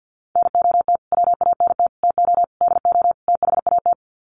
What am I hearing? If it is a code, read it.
Morse code, heard as a keyed tone, about 50 wpm, 700 Hz. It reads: NOT WANT TO DO THAT
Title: Sets of 5 Words Encore 50wpm